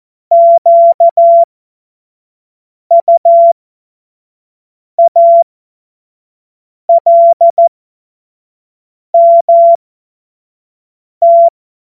Morse code, heard as a keyed tone, about 14 wpm, spaced as for 5 wpm, 680 Hz. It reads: QUALMT
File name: Morsekurs / Witze / Witz 02 April 2024 25 F70